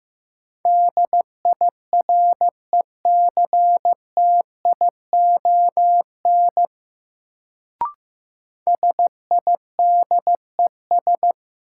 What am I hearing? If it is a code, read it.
Morse code, heard as a keyed tone, about 15 wpm, 700 Hz. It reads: DIRECTION SIDES
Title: Sets of 4 Words - New Word Drill - 15wpm